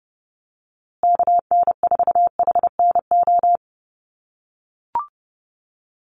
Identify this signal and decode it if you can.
Morse code, heard as a keyed tone, about 30 wpm, 700 Hz. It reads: KN4HNO
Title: USA Call Signs Repeated 3x 30wpm